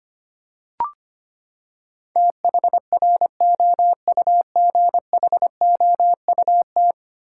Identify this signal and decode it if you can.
Morse code, heard as a keyed tone, about 25 wpm, 700 Hz. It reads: THROUGHOUT
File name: Ten Letter Words - 25wpm